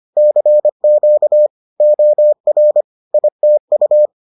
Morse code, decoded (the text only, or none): CQ OR ITU